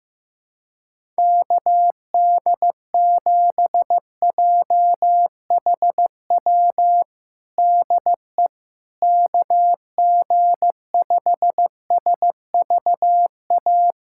Morse code, heard as a keyed tone, about 16 wpm, 700 Hz. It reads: KD7JHW DE KG5SVA
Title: USA Callsign DE Callsign - 6 Characters - 15wpm